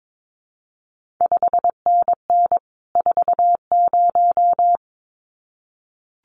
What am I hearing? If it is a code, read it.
Morse code, heard as a keyed tone, about 22 wpm, 700 Hz. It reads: 5NN 40
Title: CQ WW DX Contest Exchange Encore - 22wpm